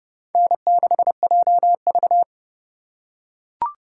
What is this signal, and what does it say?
Morse code, heard as a keyed tone, about 29 wpm, 700 Hz. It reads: N6JV